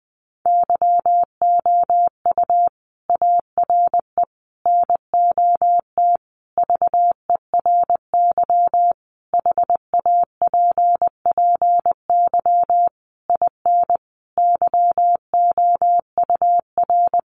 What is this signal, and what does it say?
Morse code, heard as a keyed tone, about 20 wpm, 700 Hz. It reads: YOU ARE NOT VERY HAPPY IN YOUR
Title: Sentences from Top 1200 Words 20wpm